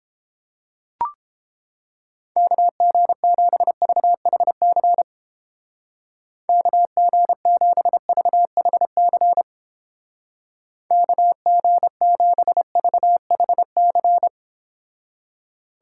Morse code, heard as a keyed tone, about 31 wpm, 700 Hz. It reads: KG7VHC KG7VHC KG7VHC